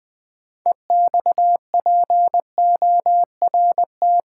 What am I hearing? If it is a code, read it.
Morse code, heard as a keyed tone, about 20 wpm, 700 Hz. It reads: EXPORT
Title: 6 Letter Words - First 3 Letters Given - 20wpm